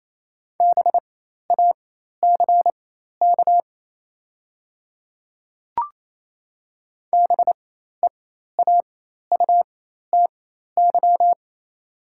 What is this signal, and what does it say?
Morse code, heard as a keyed tone, about 28 wpm, spaced as for 13 wpm, 700 Hz. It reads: BACK BEAUTY
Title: 4x Character Spacing - Top 500 Words - 28wpm